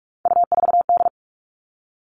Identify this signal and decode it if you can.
Morse code, heard as a keyed tone, about 44 wpm, 700 Hz. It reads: U4D
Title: All Cut Numbers - 3-Digit 45wpm